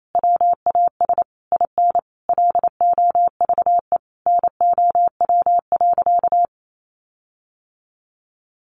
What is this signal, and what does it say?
Morse code, heard as a keyed tone, about 28 wpm, 700 Hz. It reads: WAS IN LOVE NOW.
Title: Sentences from Top 1300 Words 28wpm